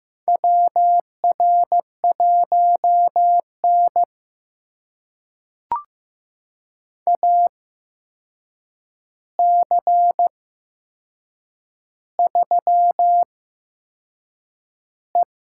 Morse code, heard as a keyed tone, about 15 wpm, spaced as for 6 wpm, 700 Hz. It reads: WR1N AC3E
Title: USA Call Signs - 4 Characters - 8x Character Spacing - 15wpm